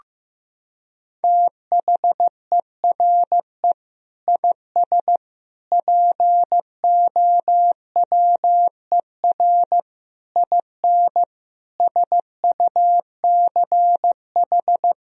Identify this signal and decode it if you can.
Morse code, heard as a keyed tone, about 16 wpm, 700 Hz. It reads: THERE IS POWER IN SUCH
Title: Sentences from Top 1000 Words - Review Part 4 - 15wpm